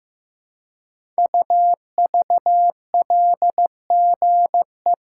Morse code, heard as a keyed tone, about 15 wpm, 700 Hz. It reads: UVLGE
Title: Code Groups - 5 Letter:Numbers - 15wpm